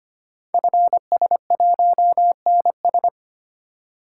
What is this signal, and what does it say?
Morse code, heard as a keyed tone, about 25 wpm, 700 Hz. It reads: FS1NS
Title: Code Groups - 5 Letter:Numbers - 25wpm